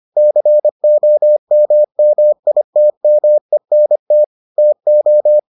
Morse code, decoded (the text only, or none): COMMITMENT TO